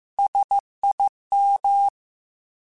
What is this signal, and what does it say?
Morse code, heard as a keyed tone, about 15 wpm, 800 Hz. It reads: SIM
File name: S vs H / S vs H mp3s / S vs H @13